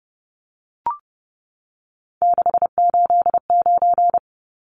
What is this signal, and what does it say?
Morse code, heard as a keyed tone, about 30 wpm, 700 Hz. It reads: 689
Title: Numbers - 3-Digit 30wpm